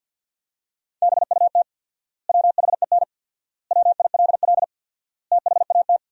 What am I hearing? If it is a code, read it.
Morse code, heard as a keyed tone, about 52 wpm, 700 Hz. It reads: BUT WHEN WILL THAT
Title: 4x Word-Spacing - Sentences from Top 100 Words - 50wpm